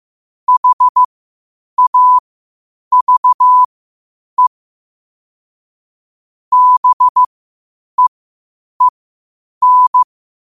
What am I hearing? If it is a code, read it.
Morse code, heard as a keyed tone, about 15 wpm, spaced as for 8 wpm, 1000 Hz. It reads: HAVE BEEN
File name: qotdfiles / 8wpm